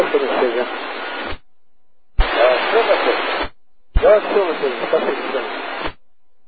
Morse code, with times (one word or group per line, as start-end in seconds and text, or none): none